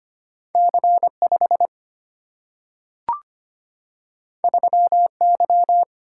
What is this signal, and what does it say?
Morse code, heard as a keyed tone, about 25 wpm, 700 Hz. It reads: C 5 3 Y
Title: Single-Letter Single-Number 25wpm